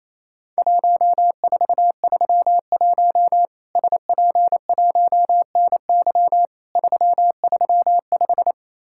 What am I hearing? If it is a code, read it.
Morse code, heard as a keyed tone, about 28 wpm, 700 Hz. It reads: 1431 SP1NY 335
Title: Worked All Europe DX Contest - QTC Exchange - 28wpm